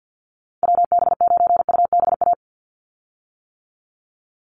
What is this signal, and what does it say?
Morse code, heard as a keyed tone, about 50 wpm, 700 Hz. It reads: WB9UBA
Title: USA Call Signs 50wpm